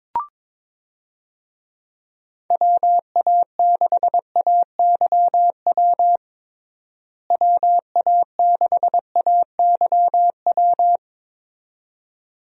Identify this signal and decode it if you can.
Morse code, heard as a keyed tone, about 22 wpm, 700 Hz. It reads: WA6AYW WA6AYW
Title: USA Call Signs - 6 Characters - Repeated 2x - 22wpm